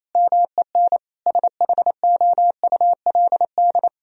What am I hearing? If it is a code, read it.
Morse code, heard as a keyed tone, about 28 wpm, 700 Hz. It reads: MEN SHOULD
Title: Sentences from Top 1000 Words - Review Part 1 - 28wpm